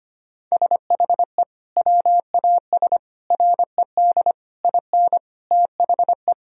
Morse code, heard as a keyed tone, about 25 wpm, 700 Hz. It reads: SHE WAS RED IN THE
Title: Sentences from Top 1000 Words - Review Part 2 - 25wpm